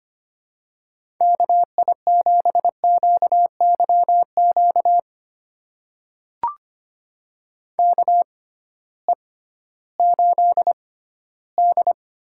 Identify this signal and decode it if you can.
Morse code, heard as a keyed tone, about 25 wpm, spaced as for 15 wpm, 700 Hz. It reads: KI7QYQ KE8D
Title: USA Call Signs - 6 Characters - 6x Character Spacing - 25wpm